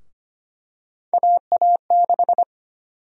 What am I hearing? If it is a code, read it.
Morse code, heard as a keyed tone, about 25 wpm, 700 Hz. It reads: AA6
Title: USA Call Sign Prefixes - Repeated 2x - 25wpm